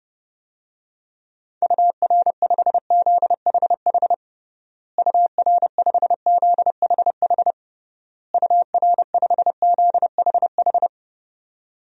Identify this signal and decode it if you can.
Morse code, heard as a keyed tone, about 30 wpm, 700 Hz. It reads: UR5ZHH UR5ZHH UR5ZHH
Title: DXCC-100 Call Signs - 6 Characters - Repeated 3x - 30wpm